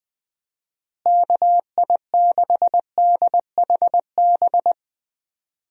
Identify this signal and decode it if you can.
Morse code, heard as a keyed tone, about 21 wpm, 700 Hz. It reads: KI6DHB
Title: USA Call Signs - 6 Characters - Repeated 3x - 20wpm